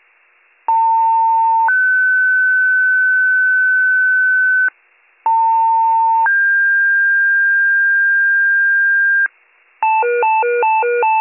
Two-tone page, heard as a one-second tone, then a three-second tone, about 900 then 1500 Hz, about 900 then 1600 Hz. It is heard from 0.7 to 9.3 s.